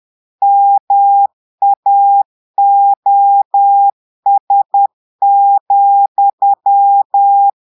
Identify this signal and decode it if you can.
Morse code, heard as a keyed tone, about 10 wpm, 800 Hz. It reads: MAOS,
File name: Portugais - texte en clair 1 - vitesse - 10 mots minute